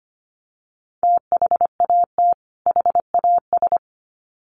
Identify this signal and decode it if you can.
Morse code, heard as a keyed tone, about 25 wpm, 700 Hz. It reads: THAT HAS
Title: Sets of 2 Words 25wpm